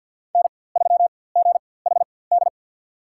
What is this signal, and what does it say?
Morse code, heard as a keyed tone, about 53 wpm, spaced as for 24 wpm, 700 Hz. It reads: N3GHD
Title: USA Call Signs - 5 Characters - 4x Character Spacing - 50wpm